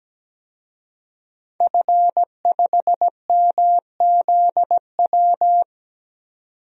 Morse code, heard as a keyed tone, about 17 wpm, 700 Hz. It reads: F5MZW
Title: DXCC-100 Call Signs - 5 Characters - 17wpm